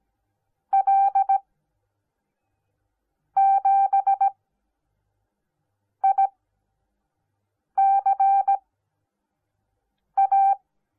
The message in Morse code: L7ICA

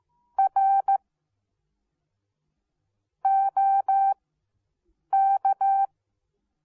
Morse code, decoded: R OK

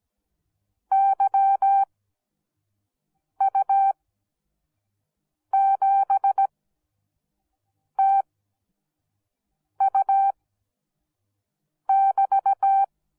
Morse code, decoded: YU7TU<BT>

